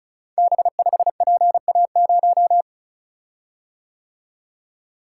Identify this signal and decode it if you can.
Morse code, heard as a keyed tone, about 36 wpm, 700 Hz. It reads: B5PA0
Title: Code Groups - 5 Letter:Numbers - 35wpm